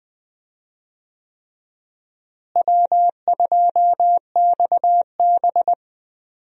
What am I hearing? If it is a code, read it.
Morse code, heard as a keyed tone, about 20 wpm, 700 Hz. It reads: W2XB
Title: USA Call Signs - 4 Characters - Repeated 3x - 20wpm